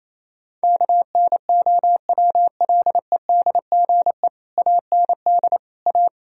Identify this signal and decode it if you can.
Morse code, heard as a keyed tone, about 28 wpm, 700 Hz. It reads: KNOWLEDGE AND A
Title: Sentences from Top 1000 Words - Review Part 4 - 28wpm